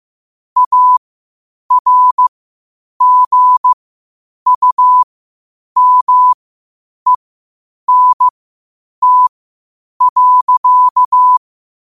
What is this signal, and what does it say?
Morse code, heard as a keyed tone, about 15 wpm, spaced as for 8 wpm, 1000 Hz. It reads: ARGUMENT.
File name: qotdfiles / 8wpm